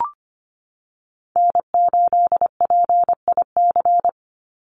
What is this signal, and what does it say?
Morse code, heard as a keyed tone, about 25 wpm, 700 Hz. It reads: N8PIC
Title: USA Call Signs 25wpm